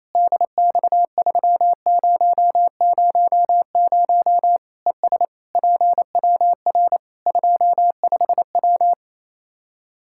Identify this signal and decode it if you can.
Morse code, heard as a keyed tone, about 28 wpm, 700 Hz. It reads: DX3000 ES PWR 25W
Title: K1USN SST QSO Practice 28wpm